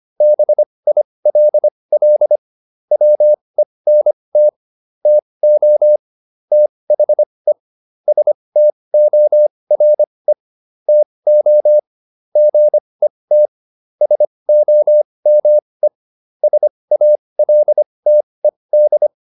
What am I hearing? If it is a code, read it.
Morse code, heard as a keyed tone, about 25 wpm, spaced as for 20 wpm, 600 Hz. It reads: BILL WENT TO THE STORE TO GET SOME SALTED